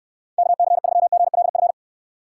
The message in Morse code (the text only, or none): KC3GLF